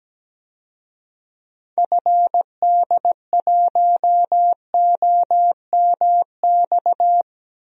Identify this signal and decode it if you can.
Morse code, heard as a keyed tone, about 17 wpm, 700 Hz. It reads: FD1OMX
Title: DXCC-100 Call Signs - 6 Characters - Repeated 2x - 17wpm